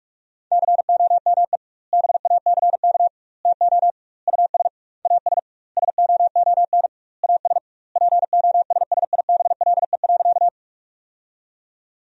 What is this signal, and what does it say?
Morse code, heard as a keyed tone, about 46 wpm, 700 Hz. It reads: COME BACK TO US AS SOON AS POSSIBLE.
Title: 2x Word-Spacing - Sentences from Top 500 Words - 45wpm